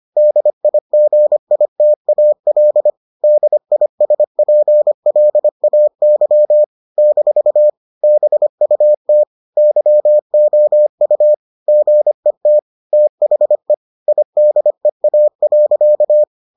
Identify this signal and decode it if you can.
Morse code, heard as a keyed tone, about 25 wpm, 600 Hz. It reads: DIGITAL DISPLAY - BUT YOU GET THE IDEA.